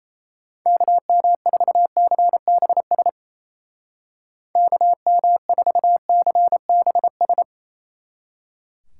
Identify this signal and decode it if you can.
Morse code, heard as a keyed tone, about 35 wpm, 700 Hz. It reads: KM4CBS KM4CBS
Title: Speed Racing - USA 2x3 Call Signs 28wpm